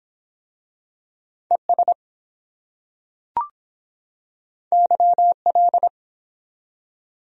Morse code, heard as a keyed tone, about 23 wpm, 700 Hz. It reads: ES YL